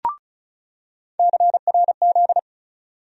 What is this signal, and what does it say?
Morse code, heard as a keyed tone, about 36 wpm, 700 Hz. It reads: CRZ
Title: USA Call Sign Postfixes - 3 Letters 35wpm